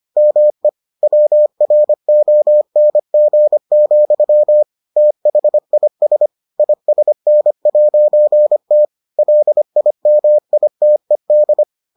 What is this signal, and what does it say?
Morse code, heard as a keyed tone, about 25 wpm, 600 Hz. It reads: ME WRONG, THIS ISN'T LIMITED